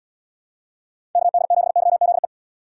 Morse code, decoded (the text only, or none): KD7YZE